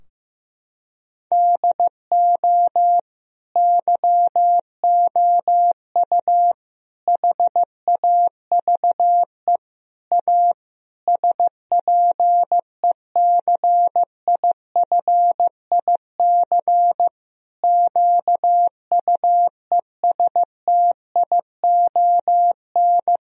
Morse code, heard as a keyed tone, about 15 wpm, 700 Hz. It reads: DO YOU HAVE A SPECIFIC QUESTION
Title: Sentences from Top 1000 Words - Review Part 3 - 15wpm